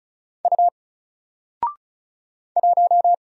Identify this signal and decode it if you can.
Morse code, heard as a keyed tone, about 34 wpm, 700 Hz. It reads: U 1
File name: Single-Letter-Number-35wpm